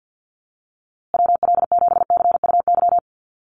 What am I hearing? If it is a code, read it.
Morse code, heard as a keyed tone, about 50 wpm, 700 Hz. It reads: WL7CUY
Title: USA Call Signs Repeated 2x 50wpm